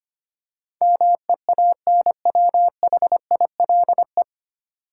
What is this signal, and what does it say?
Morse code, heard as a keyed tone, about 25 wpm, 700 Hz. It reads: MEANWHILE
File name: practice / 25wpm / 9 Letter Words - First 3 Letters Given - 25wpm